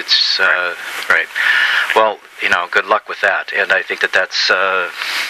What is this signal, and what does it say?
Emergency Alert System (SAME) tones, heard as harsh buzzing bursts, with speech. It is heard from 0.8 to 4.5 s.